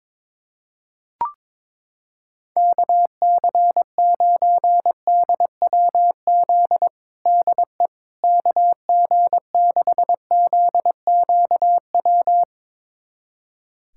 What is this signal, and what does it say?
Morse code, heard as a keyed tone, about 22 wpm, 700 Hz. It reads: KC9DWZ DE KG6ZQW
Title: USA Callsign DE Callsign - 6 Characters - 22wpm